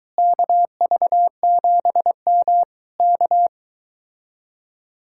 Morse code, heard as a keyed tone, about 22 wpm, 700 Hz. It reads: KV7M K